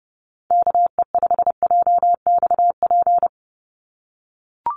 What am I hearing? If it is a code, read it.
Morse code, heard as a keyed tone, about 30 wpm, 700 Hz. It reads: KE5JXP
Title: USA Call Signs 30wpm